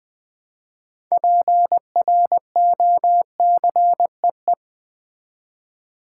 Morse code, heard as a keyed tone, about 20 wpm, 700 Hz. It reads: PROCEE
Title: Build A Word - 2 Letters At A Time - 8-Letter Words - 20wpm